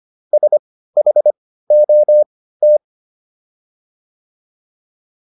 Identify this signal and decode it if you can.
Morse code, heard as a keyed tone, about 25 wpm, spaced as for 15 wpm, 600 Hz. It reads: SHOT